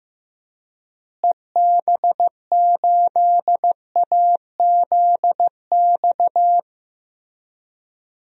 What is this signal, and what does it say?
Morse code, heard as a keyed tone, about 15 wpm, 700 Hz. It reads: EB8AZX
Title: DXCC-100 Call Signs - 6 Characters - Repeated 2x - 15wpm